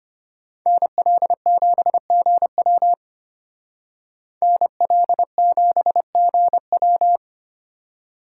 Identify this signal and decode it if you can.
Morse code, heard as a keyed tone, about 33 wpm, 700 Hz. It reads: NL7GW NL7GW